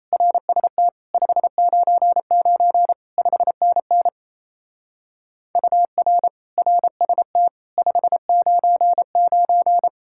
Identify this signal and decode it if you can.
Morse code, heard as a keyed tone, about 34 wpm, 700 Hz. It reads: RST 599 5NN UR RST 599
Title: Speed-Racing Common QSO Phrases - 28wpm